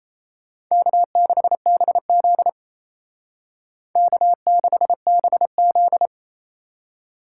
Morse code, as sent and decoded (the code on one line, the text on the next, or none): -.- -.... -... --.. / -.- -.... -... --..
K6BZ K6BZ